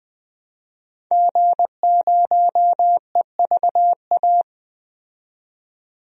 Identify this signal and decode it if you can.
Morse code, heard as a keyed tone, about 20 wpm, 700 Hz. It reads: G0EVA
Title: DXCC-100 Call Signs - 5 Characters - 20wpm